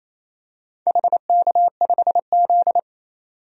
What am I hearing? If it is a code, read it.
Morse code, heard as a keyed tone, about 28 wpm, 700 Hz. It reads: HK5Z
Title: DXCC-100 Call Signs - 4 Characters - 28wpm